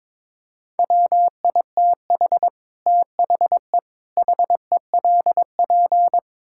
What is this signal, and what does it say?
Morse code, heard as a keyed tone, about 22 wpm, 700 Hz. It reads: WITH THE HELP
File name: practice / 22wpm / Build A Phrase - 5-Word Phrases - 22wpm